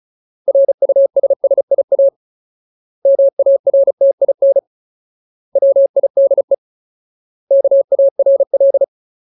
Morse code, heard as a keyed tone, about 34 wpm, 550 Hz. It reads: RUSSIA MARTIN WIDE KARL
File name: Ultimate Word List / 35WPM Ultimate